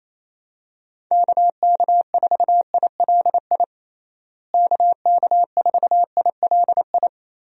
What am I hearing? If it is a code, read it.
Morse code, heard as a keyed tone, about 27 wpm, 700 Hz. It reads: KK4ILI KK4ILI
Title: USA Call Signs - 6 Characters - Repeated 3x - 28wpm